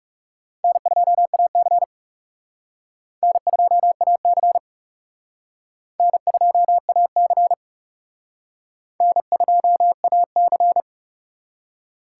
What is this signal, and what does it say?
Morse code, heard as a keyed tone, about 39 wpm, 700 Hz. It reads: N2AC N2AC N2AC N2AC